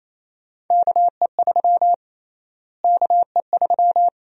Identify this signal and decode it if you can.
Morse code, heard as a keyed tone, about 27 wpm, 700 Hz. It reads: KE3 KE3
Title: USA Call Sign Prefixes - Repeated 2x - 28wpm